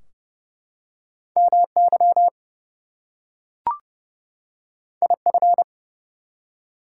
Morse code, heard as a keyed tone, about 29 wpm, 700 Hz. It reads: MY IF
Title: Two Letter Words - 30wpm